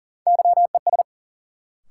Morse code, heard as a keyed tone, about 40 wpm, 700 Hz. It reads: YES
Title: Sentences from Top 500 Words - New Word Drill - 40wpm